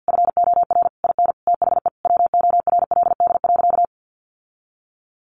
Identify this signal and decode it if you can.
Morse code, heard as a keyed tone, about 50 wpm, 700 Hz. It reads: FOR IN THE WORLD.
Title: Sentences from Top 1100 Words 50wpm